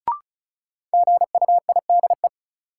Morse code, heard as a keyed tone, about 36 wpm, 700 Hz. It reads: GUIDE
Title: Sentences from Top 600 Words - New Word Drill - 35wpm